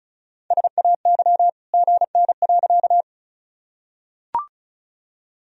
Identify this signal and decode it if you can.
Morse code, heard as a keyed tone, about 34 wpm, 700 Hz. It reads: SAY GN.
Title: Sentences from Top 300 Words and Abbreviations 35wpm